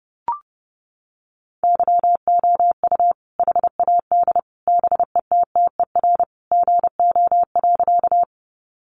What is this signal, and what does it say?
Morse code, heard as a keyed tone, about 30 wpm, 700 Hz. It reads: YOU HAD BETTER GO.
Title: Sentences from Top 400 Words 30wpm